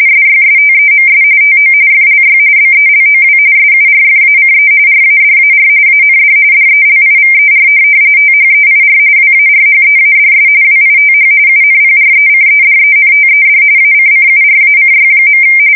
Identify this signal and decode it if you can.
RTTY, about 45.45 baud. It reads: SYSTEMS TO FLY BELOW SURFACE RADAR AND DELIVER THEIR WARHEADS WITH UNPRECEDENTED ACCURACY.
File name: fsk170 / USS-New-Jersey